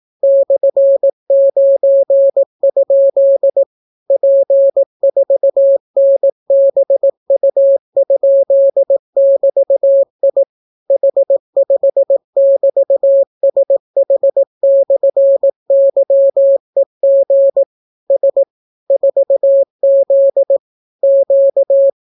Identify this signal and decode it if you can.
Morse code, heard as a keyed tone, about 18 wpm, 550 Hz. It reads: /9? P4NBU?<BT>I H5<BT>SH/YEG S 4Z Q